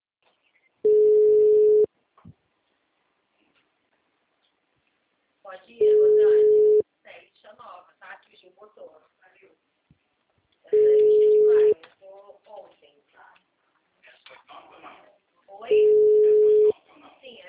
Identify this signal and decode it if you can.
Ringback tone (Europe, 425 Hz), with speech.
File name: out- -2016- - - .42693